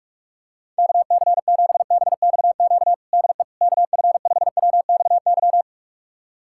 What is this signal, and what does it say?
Morse code, heard as a keyed tone, about 41 wpm, 700 Hz. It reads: KC7BXQ DE KF5WXY
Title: USA Callsign DE Callsign 45wpm